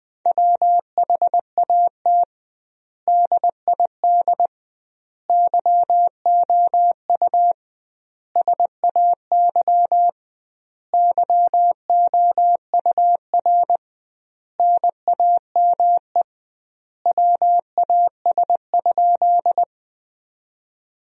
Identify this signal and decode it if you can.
Morse code, heard as a keyed tone, about 20 wpm, 700 Hz. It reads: WHAT DID YOU SAY YOUR NAME WAS?
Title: 2x Word-Spacing - Sentences from Top 200 Words - 20wpm